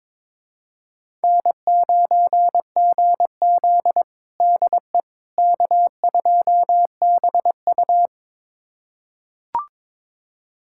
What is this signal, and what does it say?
Morse code, heard as a keyed tone, about 22 wpm, 700 Hz. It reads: N9GZ DE K2BU
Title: USA Callsign DE Callsign - 4 Characters - 22wpm